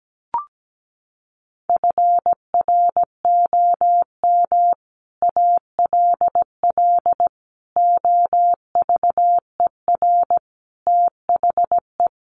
Morse code, heard as a keyed tone, about 17 wpm, 700 Hz. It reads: FROM ALL OVER THE
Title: Sets of 5 Words 17wpm